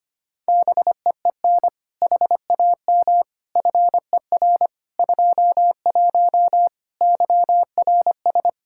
Morse code, heard as a keyed tone, about 25 wpm, 700 Hz. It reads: BEEN HAM FER 21 YRS
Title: Practice QSO - Exchange 3 - More Info - 25wpm